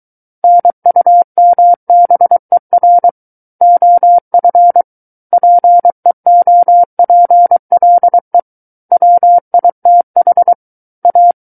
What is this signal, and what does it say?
Morse code, heard as a keyed tone, about 23 wpm, 700 Hz. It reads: NUMBER OF PEOPLE WITH A
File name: Book One / 14-Chapter 14 - 23 WPM